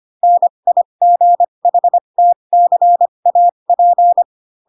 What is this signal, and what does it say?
Morse code, heard as a keyed tone, about 25 wpm, spaced as for 22 wpm, 700 Hz. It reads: NIGHTCAP
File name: Pinocchio / Pinocchio ch6 18 25 ik1hge